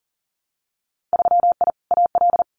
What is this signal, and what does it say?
Morse code, heard as a keyed tone, about 40 wpm, 700 Hz. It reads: 3I AL
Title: Winter Field Day Exchanges Encore 40wpm